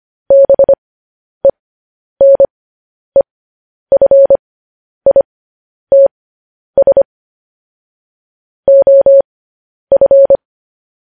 Morse code, heard as a keyed tone, about 25 wpm, spaced as for 10 wpm, 550 Hz. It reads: BENEFITS OF
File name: podcast / News headlines-1x-25x10